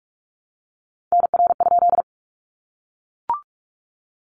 Morse code, heard as a keyed tone, about 44 wpm, 700 Hz. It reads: NR?
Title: Common QSO Phrases 45wpm